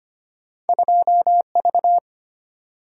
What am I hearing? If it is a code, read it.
Morse code, heard as a keyed tone, about 25 wpm, 700 Hz. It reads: 2V
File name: Two Letter-Number Combinations - 25wpm